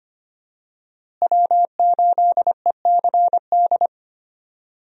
Morse code, heard as a keyed tone, about 25 wpm, 700 Hz. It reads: W8ECD
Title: USA Call Signs - 5 Characters - Repeated 3x - 25wpm